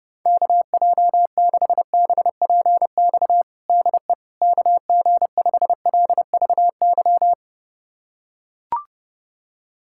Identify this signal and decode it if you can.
Morse code, heard as a keyed tone, about 30 wpm, 700 Hz. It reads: KJ6BPX DE KG5LVY
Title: USA Callsign DE Callsign - 6 Characters - 30wpm